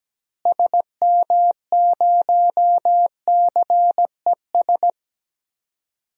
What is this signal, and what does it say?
Morse code, heard as a keyed tone, about 17 wpm, 700 Hz. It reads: SM0CES
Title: DXCC-100 Call Signs - 6 Characters - Repeated 2x - 17wpm